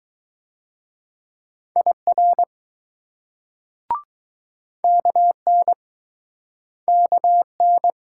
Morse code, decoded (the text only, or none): IR KN KN